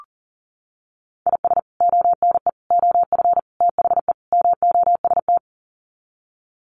Morse code, heard as a keyed tone, about 40 wpm, 700 Hz. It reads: IS ONE OF THE MOST
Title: Sets of 5 Words 40wpm